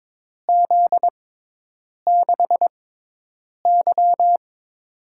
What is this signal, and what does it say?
Morse code, heard as a keyed tone, about 22 wpm, spaced as for 8 wpm, 700 Hz. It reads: Z6Y
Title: USA Call Signs - 6x Character Spacing - 22wpm